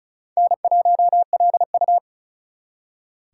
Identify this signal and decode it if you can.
Morse code, heard as a keyed tone, about 47 wpm, 700 Hz. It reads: N1LU